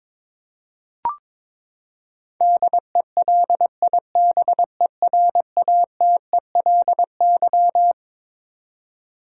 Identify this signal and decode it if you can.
Morse code, heard as a keyed tone, about 21 wpm, 700 Hz. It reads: DELIBERATELY
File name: Twelve Letter Words - 22wpm